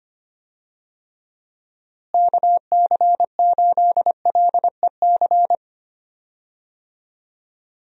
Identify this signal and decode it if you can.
Morse code, heard as a keyed tone, about 25 wpm, 700 Hz. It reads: KC8LEC
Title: USA Call Signs - 6 Characters - Repeated 3x - 25wpm